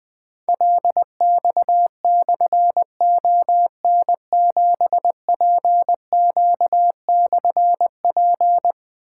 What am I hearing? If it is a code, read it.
Morse code, heard as a keyed tone, about 21 wpm, 700 Hz. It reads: LX/ON7PQ/P
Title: Difficult Call Signs 20wpm